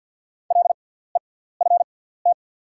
Morse code, heard as a keyed tone, about 48 wpm, spaced as for 17 wpm, 690 Hz. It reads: LEFT
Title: 6x Character Spacing - Top 300 Words - 50wpm